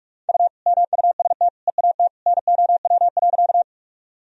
Morse code, heard as a keyed tone, about 46 wpm, 700 Hz. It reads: U MUST EAT NOW.
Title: Sentences from Top 300 Words and Abbreviations 45wpm